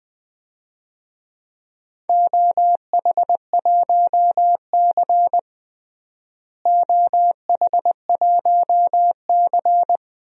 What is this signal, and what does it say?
Morse code, heard as a keyed tone, about 20 wpm, 700 Hz. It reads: OH1C OH1C
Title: DXCC-100 Call Signs - 4 Characters - Repeated 3x - 20wpm